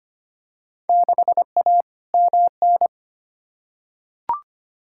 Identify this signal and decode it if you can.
Morse code, heard as a keyed tone, about 25 wpm, 700 Hz. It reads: 6A MN
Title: Field Day Exchanges Encore - 25wpm